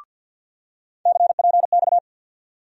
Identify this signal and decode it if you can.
Morse code, heard as a keyed tone, about 53 wpm, 700 Hz. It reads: CPX
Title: USA Call Sign Postfixes - 3 Letters 50wpm